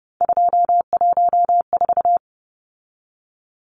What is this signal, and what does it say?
Morse code, heard as a keyed tone, about 30 wpm, 700 Hz. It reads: 214
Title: Numbers - 3-Digit 30wpm